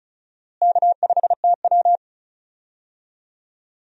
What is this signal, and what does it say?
Morse code, heard as a keyed tone, about 36 wpm, 700 Hz. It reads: K5TW
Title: USA Call Signs - 4 Characters - Repeated 3x - 35wpm